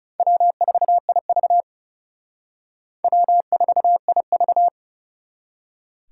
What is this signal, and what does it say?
Morse code, heard as a keyed tone, about 39 wpm, 700 Hz. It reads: W4IV W4IV